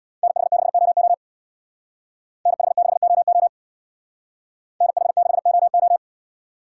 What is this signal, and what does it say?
Morse code, heard as a keyed tone, about 64 wpm, 700 Hz. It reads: NH6OO NH6OO NH6OO